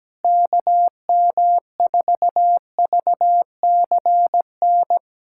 Morse code, heard as a keyed tone, about 17 wpm, 700 Hz. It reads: KM4VCN
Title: USA Callsign DE Callsign - 6 Characters - 17wpm